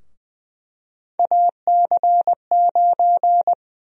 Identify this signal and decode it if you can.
Morse code, heard as a keyed tone, about 20 wpm, 700 Hz. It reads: AC9
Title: USA Call Sign Prefixes - 20wpm